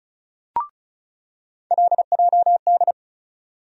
Morse code, heard as a keyed tone, about 34 wpm, 700 Hz. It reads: LJD
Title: USA Call Sign Postfixes - 3 Letters 35wpm